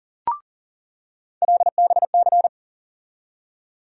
Morse code, heard as a keyed tone, about 40 wpm, 700 Hz. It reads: LBC
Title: USA Call Sign Postfixes - 3 Letters 40wpm